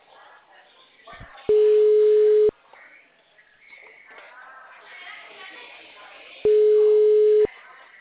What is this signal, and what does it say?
Ringback tone (Europe, 425 Hz), with speech.
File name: out- -2003- - - .10736